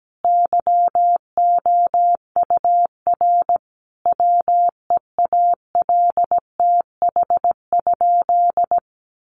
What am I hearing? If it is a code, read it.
Morse code, heard as a keyed tone, about 17 wpm, 700 Hz. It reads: YOUR WEALTH?
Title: Sentences from Top 1200 Words 17wpm